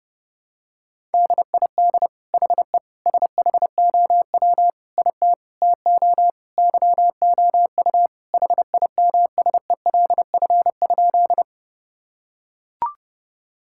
Morse code, heard as a keyed tone, about 30 wpm, 700 Hz. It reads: DID HE SHOW IT TO YOU HIMSELF?
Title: Sentences from Top 1600 Words 30wpm